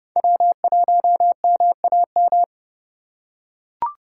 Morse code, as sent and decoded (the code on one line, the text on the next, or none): .-- .---- -- .- --
W1MAM